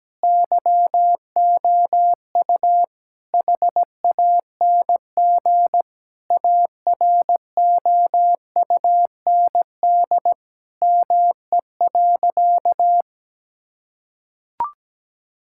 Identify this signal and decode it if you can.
Morse code, heard as a keyed tone, about 17 wpm, 700 Hz. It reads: YOU HANG AROUND ME.
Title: Sentences from Top 1600 Words 17wpm